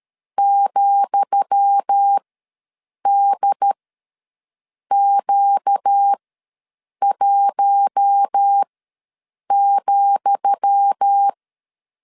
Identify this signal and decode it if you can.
Morse code, heard as a keyed tone, about 13 wpm, spaced as for 7 wpm, 800 Hz. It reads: ,DQ1,
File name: LEZIONI / 10 SIGNS / 10D COMA ws 12wpm 3618